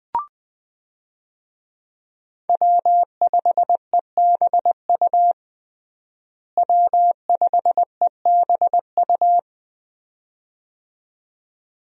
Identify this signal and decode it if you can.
Morse code, heard as a keyed tone, about 20 wpm, 700 Hz. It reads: W5EBU W5EBU
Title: USA Call Signs - 5 Characters - Repeated 2x - 20wpm